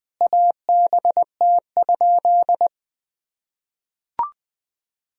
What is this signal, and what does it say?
Morse code, heard as a keyed tone, about 20 wpm, 700 Hz. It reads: ABT?
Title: Sentences from Top 100 Words and Abbreviations 20wpm